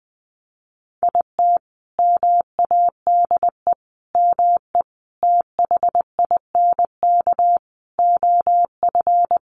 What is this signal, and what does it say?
Morse code, heard as a keyed tone, about 20 wpm, 700 Hz. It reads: IT MADE ME THINK OF
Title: Sentences from Top 1100 Words 20wpm